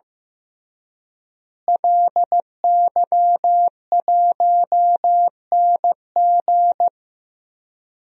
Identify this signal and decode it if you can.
Morse code, heard as a keyed tone, about 15 wpm, 700 Hz. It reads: LY1NG
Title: DXCC-100 Call Signs - 5 Characters - Repeated 3x - 15wpm